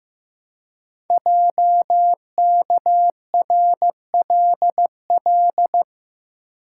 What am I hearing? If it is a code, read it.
Morse code, heard as a keyed tone, about 15 wpm, 700 Hz. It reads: JKRLL
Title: Code Groups - 5 Letters - 15wpm